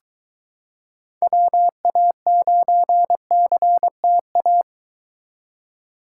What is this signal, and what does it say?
Morse code, heard as a keyed tone, about 23 wpm, 700 Hz. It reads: WA9CTA